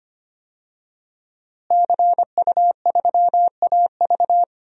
Morse code, heard as a keyed tone, about 25 wpm, 700 Hz. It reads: CU3AV
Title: DXCC-100 Call Signs - 5 Characters - Repeated 2x - 25wpm